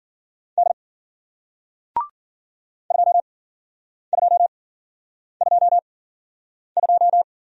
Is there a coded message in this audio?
Yes: Morse code from 0.6 to 7.2 s, heard as a keyed tone, about 54 wpm, spaced as for 9 wpm, 700 Hz. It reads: D 2222